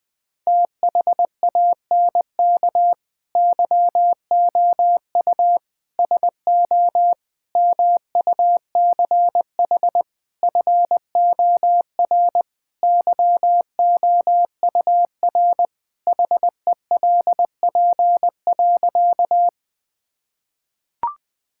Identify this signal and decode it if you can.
Morse code, heard as a keyed tone, about 20 wpm, 700 Hz. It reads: THANK YOU SO MUCH FOR YOUR HELP.
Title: Sentences with 5-Word Phrases - 20wpm